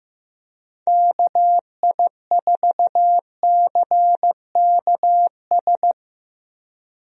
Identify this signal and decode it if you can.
Morse code, heard as a keyed tone, about 15 wpm, 700 Hz. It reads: KI4CKS
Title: USA Call Signs - 6 Characters - Repeated 3x - 15wpm